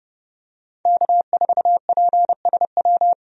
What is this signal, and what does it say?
Morse code, heard as a keyed tone, about 30 wpm, 700 Hz. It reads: K4PSW
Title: USA Call Signs - 5 Characters - 30wpm